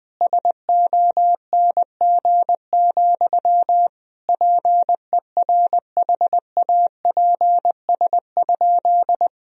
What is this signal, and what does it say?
Morse code, heard as a keyed tone, about 20 wpm, 700 Hz. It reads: SONG, PERHAPS?
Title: Sentences from Top 1000 Words - Review Part 2 - 20wpm